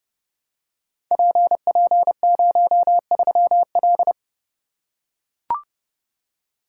Morse code, heard as a keyed tone, about 29 wpm, 700 Hz. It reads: PP03L
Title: Code Groups - 5 Letter:Numbers - 30wpm